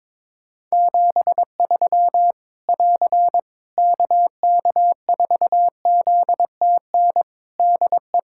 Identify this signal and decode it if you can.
Morse code, heard as a keyed tone, about 22 wpm, 700 Hz. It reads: 73 <AR> KK4ZTN DE
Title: Practice QSO - Exchange 4 - Ending - 22wpm